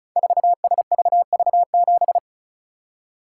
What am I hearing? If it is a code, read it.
Morse code, heard as a keyed tone, about 36 wpm, 700 Hz. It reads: 4SVV7
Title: Code Groups - 5 Letter:Numbers - 35wpm